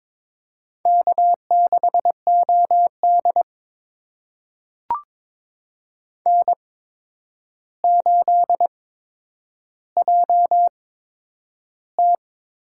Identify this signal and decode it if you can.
Morse code, heard as a keyed tone, about 22 wpm, spaced as for 9 wpm, 700 Hz. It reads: K6OD N8JT
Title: USA Call Signs - 4 Characters - 8x Character Spacing - 22wpm